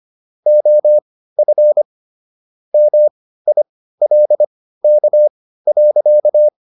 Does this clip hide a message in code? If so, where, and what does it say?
Yes: Morse code from 0.5 to 6.5 s, heard as a keyed tone, about 25 wpm, spaced as for 15 wpm, 600 Hz. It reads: OF MILK.